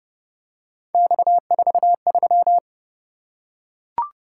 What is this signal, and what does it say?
Morse code, heard as a keyed tone, about 28 wpm, 700 Hz. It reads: X43